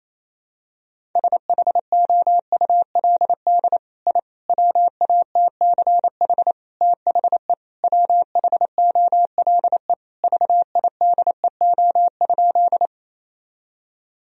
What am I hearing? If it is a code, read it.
Morse code, heard as a keyed tone, about 27 wpm, 700 Hz. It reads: SHOULD I WATCH THE WHOLE VIDEO?
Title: Sentences from Top 1000 Words - Review Part 4 - 28wpm